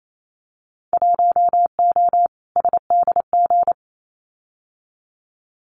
Morse code, heard as a keyed tone, about 28 wpm, 700 Hz. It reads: 1O SDG
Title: Winter Field Day Exchanges Encore 28wpm